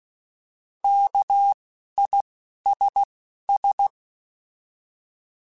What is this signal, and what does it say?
Morse code, heard as a keyed tone, about 16 wpm, 800 Hz. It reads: K I S S
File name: Alphabetical / Alphabetical mp3s / Ks @10